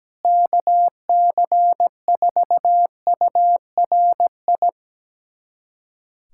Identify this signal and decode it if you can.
Morse code, heard as a keyed tone, about 17 wpm, 700 Hz. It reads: KC4URI
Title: USA Callsign DE Callsign 17wpm